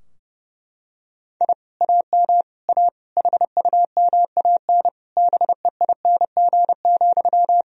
Morse code, heard as a keyed tone, about 30 wpm, 700 Hz. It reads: I AM A HUMAN BEING,
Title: Sentences from Top 1000 Words - Review Part 3 - 30wpm